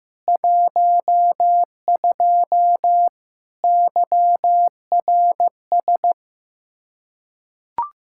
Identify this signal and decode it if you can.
Morse code, heard as a keyed tone, about 15 wpm, 700 Hz. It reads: 12 YRS
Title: Practice QSO - Exchange 3 - More Info - 15wpm